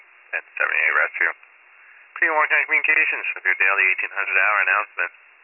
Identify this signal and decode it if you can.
Emergency Alert System (SAME) tones, heard as harsh buzzing bursts, with speech. It is heard from 0.9 to 4.8 s.